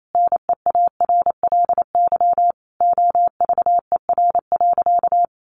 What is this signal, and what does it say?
Morse code, heard as a keyed tone, about 28 wpm, 700 Hz. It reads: NEARLY OVER.
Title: Sentences from Top 1500 Words 28wpm